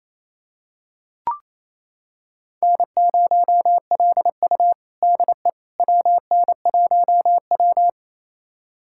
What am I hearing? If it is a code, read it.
Morse code, heard as a keyed tone, about 28 wpm, 700 Hz. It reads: N0LU DE WN1W
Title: USA Callsign DE Callsign - 4 Characters - 28wpm